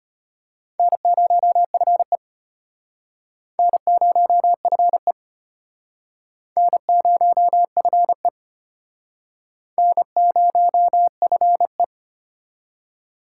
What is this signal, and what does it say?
Morse code, heard as a keyed tone, about 33 wpm, 700 Hz. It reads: N0FE N0FE N0FE N0FE